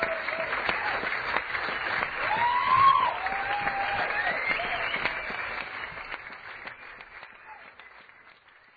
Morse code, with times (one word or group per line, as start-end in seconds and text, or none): none